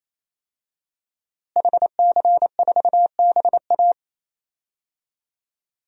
Morse code, heard as a keyed tone, about 28 wpm, 700 Hz. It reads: HC4BA
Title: DXCC-100 Call Signs - 5 Characters - Repeated 3x - 28wpm